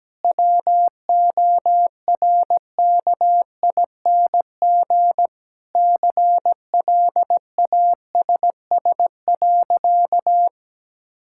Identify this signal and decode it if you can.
Morse code, heard as a keyed tone, about 17 wpm, 700 Hz. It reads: WORKING CLASS.
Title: Sentences from Top 1000 Words - Review Part 6 - 17wpm